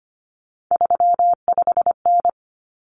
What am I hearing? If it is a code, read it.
Morse code, heard as a keyed tone, about 25 wpm, 700 Hz. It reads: 35N
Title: Common Cut Numbers - 3-Digit 25wpm